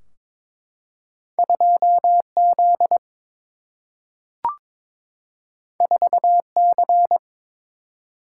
Morse code, heard as a keyed tone, about 22 wpm, 700 Hz. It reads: 2Z 4C
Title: Single-Letter Single-Number 22wpm